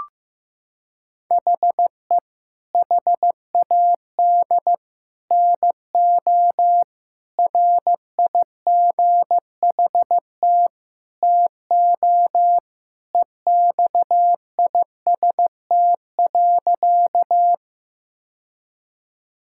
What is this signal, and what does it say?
Morse code, heard as a keyed tone, about 16 wpm, 700 Hz. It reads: HE HAD NO RIGHT TO EXIST.
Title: Sentences from Top 1000 Words - Review Part 5 - 15wpm